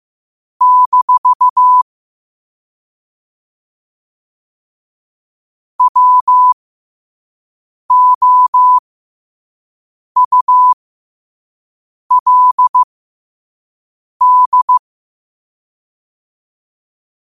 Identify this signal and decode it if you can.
Morse code, heard as a keyed tone, about 15 wpm, spaced as for 5 wpm, 1000 Hz. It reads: - WOULD
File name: qotdfiles / 5wpm